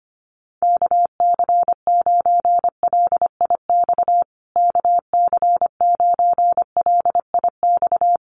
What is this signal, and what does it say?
Morse code, heard as a keyed tone, about 25 wpm, 700 Hz. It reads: KC9LIX KC9LIX
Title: USA Call Signs Repeated 3x 25wpm